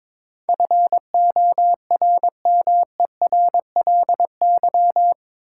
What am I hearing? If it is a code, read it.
Morse code, heard as a keyed tone, about 22 wpm, 700 Hz. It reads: FORMERLY
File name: practice / 22wpm / Eight Letter Words - 22wpm